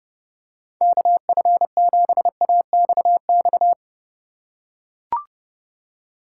Morse code, heard as a keyed tone, about 32 wpm, 700 Hz. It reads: KF7AXX
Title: USA Call Signs - 6 Characters - 4x Character Spacing - 30wpm